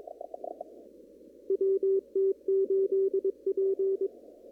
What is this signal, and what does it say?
Morse code, heard as a keyed tone, about 22 wpm, 390 Hz. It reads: WT8P
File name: CW / kt5j wpx cw-op-k5pi